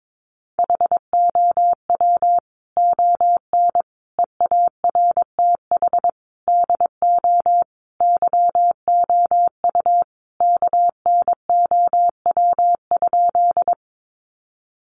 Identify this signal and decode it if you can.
Morse code, heard as a keyed tone, about 22 wpm, 700 Hz. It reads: HOW ON EARTH DO YOU KNOW?
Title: Sentences from Top 200 Words 22wpm